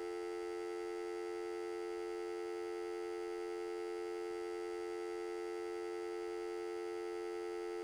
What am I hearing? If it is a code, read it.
Dial tone (North America), 350 + 440 Hz.
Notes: efectos para radio